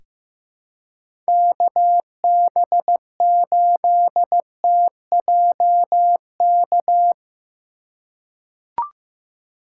Morse code, heard as a keyed tone, about 15 wpm, 700 Hz. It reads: KB8TJK
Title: USA Call Signs - 6 Characters - 4x Character Spacing - 15wpm